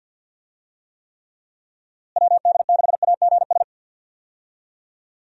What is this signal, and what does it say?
Morse code, heard as a keyed tone, about 47 wpm, 700 Hz. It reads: WD6AGS
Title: USA Call Signs - 6 Characters - Repeated 3x - 50wpm